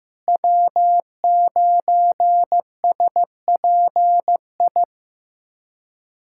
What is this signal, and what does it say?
Morse code, heard as a keyed tone, about 15 wpm, 700 Hz. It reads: W9SPI
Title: USA Callsign DE Callsign - 5 Characters - 15wpm